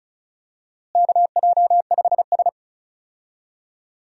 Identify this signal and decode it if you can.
Morse code, heard as a keyed tone, about 34 wpm, 700 Hz. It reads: KJ5S
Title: USA Call Signs - 4 Characters - Repeated 2x - 35wpm